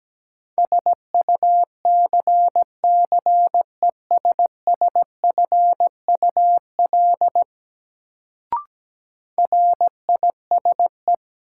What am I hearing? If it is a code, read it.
Morse code, heard as a keyed tone, about 17 wpm, 700 Hz. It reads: SUCCESSFUL RISE
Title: Sentences from Top 900 Words - New Word Drill - 17wpm